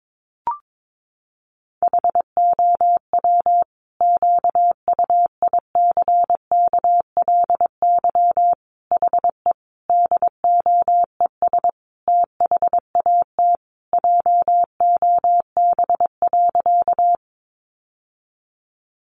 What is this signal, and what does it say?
Morse code, heard as a keyed tone, about 22 wpm, 700 Hz. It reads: HOW QUICKLY HE DOES THAT JOB.
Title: Sentences from Top 700 Words 22wpm